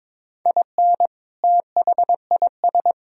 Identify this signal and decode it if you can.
Morse code, heard as a keyed tone, about 23 wpm, 700 Hz. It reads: IN THIS
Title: Speed-Racing - Sets of 3 Words 22wpm